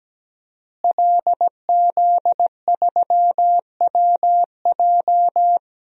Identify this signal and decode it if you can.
Morse code, heard as a keyed tone, about 17 wpm, 700 Hz. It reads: LZ3WJ
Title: DXCC-100 Call Signs - 5 Characters - Repeated 2x - 17wpm